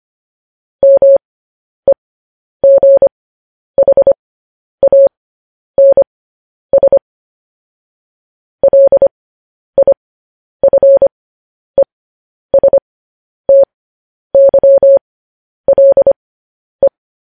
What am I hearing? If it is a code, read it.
Morse code, heard as a keyed tone, about 25 wpm, spaced as for 10 wpm, 550 Hz. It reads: MEGHANS LIFESTYLE
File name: podcast / News headlines-1x-25x10